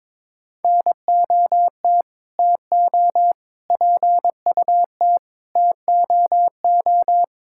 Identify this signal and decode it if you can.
Morse code, heard as a keyed tone, about 22 wpm, 700 Hz. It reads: NOT TO PUT TOO
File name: Sets of 9 Words - 22wpm